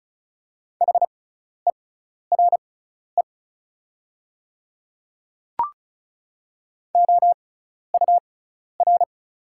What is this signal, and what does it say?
Morse code, heard as a keyed tone, about 34 wpm, spaced as for 12 wpm, 700 Hz. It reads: HERE OUR
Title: 6x Character Spacing - Top 200 Words - 35wpm